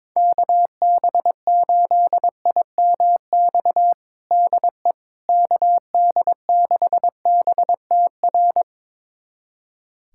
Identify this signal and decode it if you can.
Morse code, heard as a keyed tone, about 22 wpm, 700 Hz. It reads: KB8IMX DE KD6BTR
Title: USA Callsign DE Callsign 22wpm